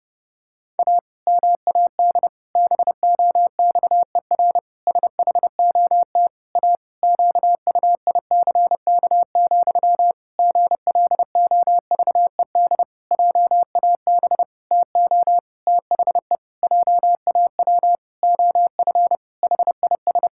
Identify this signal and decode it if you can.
Morse code, heard as a keyed tone, about 30 wpm, 700 Hz. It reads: A MAD BOXER SHOT A QUICK, GLOVED JAB TO THE JAW OF HIS
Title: Pangram Challenge v2 - 30wpm